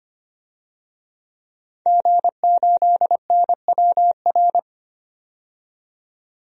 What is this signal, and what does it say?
Morse code, heard as a keyed tone, about 25 wpm, 700 Hz. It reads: G8NWR
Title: DXCC-100 Call Signs - 5 Characters - 25wpm